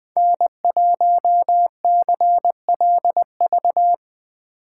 Morse code, heard as a keyed tone, about 20 wpm, 700 Hz. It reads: N1CLV
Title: Speed Racing - USA 1x3 Call Signs 17wpm